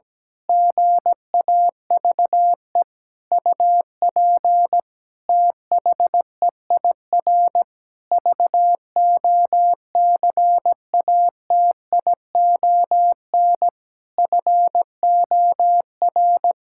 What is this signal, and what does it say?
Morse code, heard as a keyed tone, about 17 wpm, 700 Hz. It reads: GAVE UP THEIR VOCATION FOR
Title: Pangram Challenge v2 - 17wpm